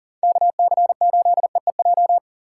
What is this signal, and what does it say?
Morse code, heard as a keyed tone, about 50 wpm, 700 Hz. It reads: KC8EEJ
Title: Speed Racing - USA 2x3 Call Signs 40wpm